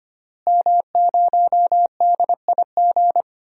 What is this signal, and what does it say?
Morse code, heard as a keyed tone, about 25 wpm, 700 Hz. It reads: M0DIG
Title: DXCC-100 Call Signs - 5 Characters - Repeated 3x - 25wpm